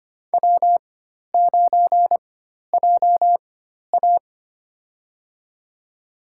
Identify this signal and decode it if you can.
Morse code, heard as a keyed tone, about 25 wpm, spaced as for 12 wpm, 700 Hz. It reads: W9JA
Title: USA Call Signs - 4 Characters - 4x Character Spacing - 25wpm